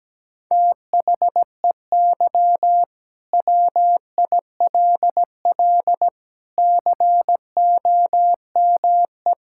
Morse code, heard as a keyed tone, about 17 wpm, 700 Hz. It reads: THEY WILL COME
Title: Sentences from Top 1600 Words 17wpm